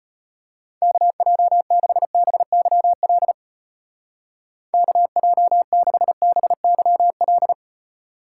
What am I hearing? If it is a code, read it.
Morse code, heard as a keyed tone, about 33 wpm, 700 Hz. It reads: KJ6BYL KJ6BYL